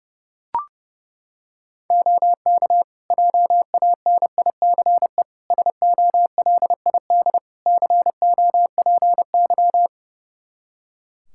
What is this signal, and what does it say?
Morse code, heard as a keyed tone, about 30 wpm, 700 Hz. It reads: OK JANICE SOLID COPY
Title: K1USN SST QSO Practice 30wpm